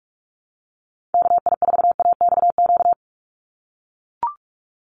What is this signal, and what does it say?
Morse code, heard as a keyed tone, about 44 wpm, 700 Hz. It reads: KI4AXQ
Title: USA Call Signs 45wpm